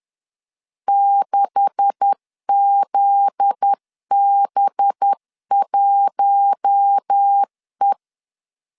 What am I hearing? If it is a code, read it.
Morse code, heard as a keyed tone, about 10 wpm, 800 Hz. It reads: 6ZB1E
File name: LEZIONI / 9 286 10wpm 2735